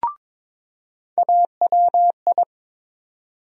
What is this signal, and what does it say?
Morse code, heard as a keyed tone, about 22 wpm, 700 Hz. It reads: AWI
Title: USA Call Sign Postfixes - 3 Letters 22wpm